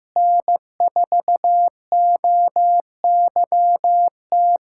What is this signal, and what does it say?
Morse code, heard as a keyed tone, about 16 wpm, 700 Hz. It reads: N4OYT
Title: USA Callsign DE Callsign - 5 Characters - 15wpm